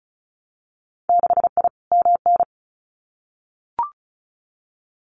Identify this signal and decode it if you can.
Morse code, heard as a keyed tone, about 36 wpm, 700 Hz. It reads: 6I MN
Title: Winter Field Day Exchanges Encore 35wpm